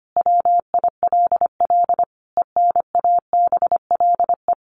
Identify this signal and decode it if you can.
Morse code, heard as a keyed tone, about 25 wpm, 700 Hz. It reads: WILL ENABLE
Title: Sentences from Top 1500 Words 25wpm